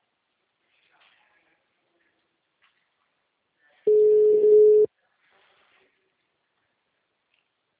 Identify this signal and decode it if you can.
Ringback tone (Europe, 425 Hz).